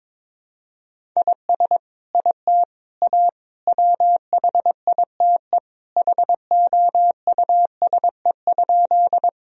Morse code, heard as a keyed tone, about 22 wpm, 700 Hz. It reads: IS IT A WHITE HOUSE?
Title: Sentences from Top 1000 Words - Review Part 1 - 22wpm